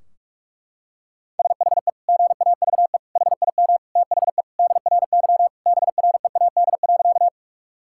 Morse code, heard as a keyed tone, about 46 wpm, 700 Hz. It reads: SHE GAVE HIM THE DRY BREAD.
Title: Sentences from Top 1600 Words 45wpm